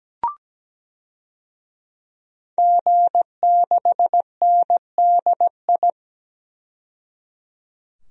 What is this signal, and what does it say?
Morse code, heard as a keyed tone, about 17 wpm, 700 Hz. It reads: G6NDI
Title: DXCC-100 Call Signs - 5 Characters - 17wpm